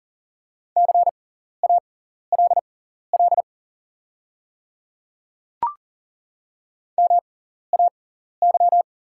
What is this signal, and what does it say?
Morse code, heard as a keyed tone, about 40 wpm, spaced as for 14 wpm, 700 Hz. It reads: CALL MAY
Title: 6x Character Spacing - Top 300 Words - 40wpm